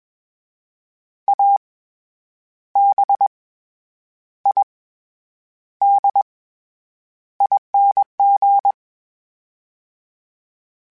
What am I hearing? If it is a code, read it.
Morse code, heard as a keyed tone, about 21 wpm, 800 Hz. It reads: A B I D ING
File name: Special INGs-mp3s / Special INGs-mp3s / SlowToFaster ing @20